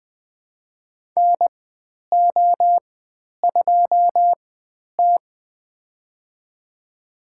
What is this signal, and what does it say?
Morse code, heard as a keyed tone, about 20 wpm, spaced as for 10 wpm, 700 Hz. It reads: NO2T
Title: 095 Lesson 16g - Letter c - Call Signs 20wpm